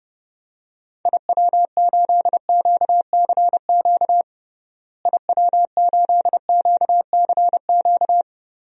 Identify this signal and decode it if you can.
Morse code, heard as a keyed tone, about 30 wpm, 700 Hz. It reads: IW8QCQ IW8QCQ
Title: DXCC-100 Call Signs - 6 Characters - Repeated 3x - 30wpm